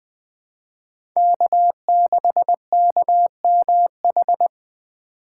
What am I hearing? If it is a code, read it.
Morse code, heard as a keyed tone, about 20 wpm, 700 Hz. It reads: K6KMH
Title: 2023 SOTA Chaser Call Signs - 20wpm